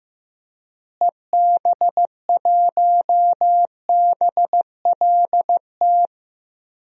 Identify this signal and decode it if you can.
Morse code, heard as a keyed tone, about 15 wpm, 700 Hz. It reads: EB1BLT
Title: DXCC-100 Call Signs - 6 Characters - Repeated 3x - 15wpm